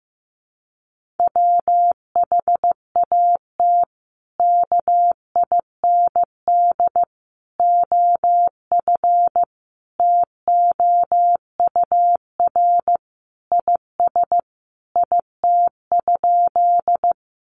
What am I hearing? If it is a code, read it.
Morse code, heard as a keyed tone, about 15 wpm, 700 Hz. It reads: WHAT KIND OF TOUR IS IT?
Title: Sentences from Top 1400 Words 15wpm